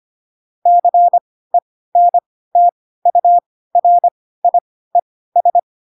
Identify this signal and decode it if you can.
Morse code, heard as a keyed tone, about 25 wpm, spaced as for 16 wpm, 700 Hz. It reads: CENTURIES
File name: Pinocchio / Pinocchio ch1 10 25 ik1hge